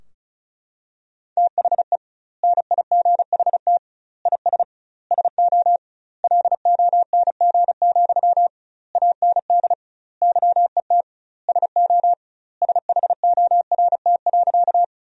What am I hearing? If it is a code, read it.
Morse code, heard as a keyed tone, about 34 wpm, 700 Hz. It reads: THE NIGHT IS SO LONG, AND YET SO SHORT.
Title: 2x Word-Spacing - Sentences from Top 500 Words - 35wpm